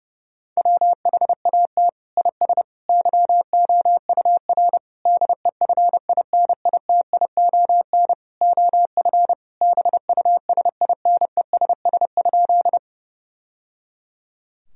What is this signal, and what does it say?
Morse code, heard as a keyed tone, about 32 wpm, 700 Hz. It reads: WHAT IS YOUR DEFINITION OF BUSINESS?
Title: Sentences from Top 1000 Words - Review Part 4 - 30wpm